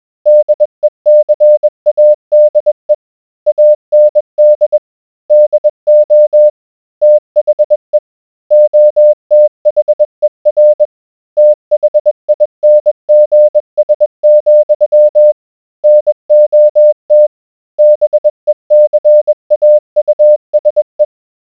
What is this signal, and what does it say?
Morse code, heard as a keyed tone, about 20 wpm, 600 Hz. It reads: DECADE AND DO THE OTHER THINGS, NOT BECAUSE